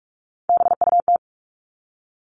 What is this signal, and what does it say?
Morse code, heard as a keyed tone, about 44 wpm, 700 Hz. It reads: BUT
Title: Top 500 Words 45wpm